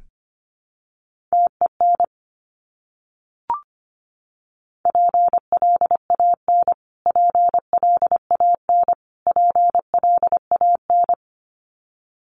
Morse code, heard as a keyed tone, about 25 wpm, 700 Hz. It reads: TEN PLAN PLAN PLAN
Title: Top 500 Words Repeated 25wpm